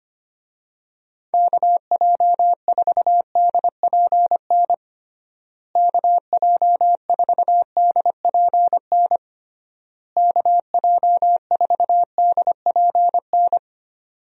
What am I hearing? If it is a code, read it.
Morse code, heard as a keyed tone, about 25 wpm, 700 Hz. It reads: KJ4DPN KJ4DPN KJ4DPN
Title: USA Call Signs - 6 Characters - Repeated 3x - 25wpm